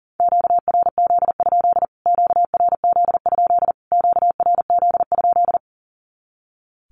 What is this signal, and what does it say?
Morse code, heard as a keyed tone, about 37 wpm, 700 Hz. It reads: QRZ? QRZ? QRZ?
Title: Q-Codes Repeated 3x 40wpm